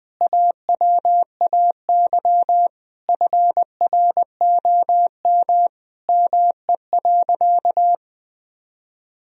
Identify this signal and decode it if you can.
Morse code, heard as a keyed tone, about 20 wpm, 700 Hz. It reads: AWAY FROM ME.
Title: Sentences from Top 1000 Words - Review Part 1 - 20wpm